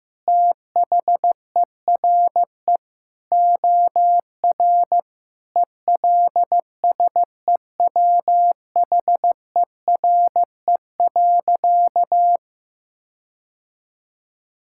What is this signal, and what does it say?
Morse code, heard as a keyed tone, about 15 wpm, 700 Hz. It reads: THERE OR ELSEWHERE.
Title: Sentences from Top 1600 Words 15wpm